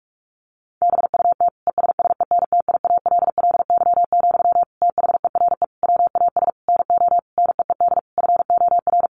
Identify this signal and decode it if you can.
Morse code, heard as a keyed tone, about 44 wpm, 700 Hz. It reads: BUT ESSENTIALLY, THERE WAS NO NEED FOR
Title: Sentences from Top 1300 Words 45wpm